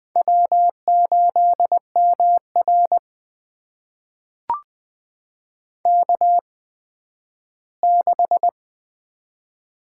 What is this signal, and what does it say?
Morse code, heard as a keyed tone, about 20 wpm, spaced as for 12 wpm, 700 Hz. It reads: W8MR K6
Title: USA Call Signs - 4 Characters - 8x Character Spacing - 20wpm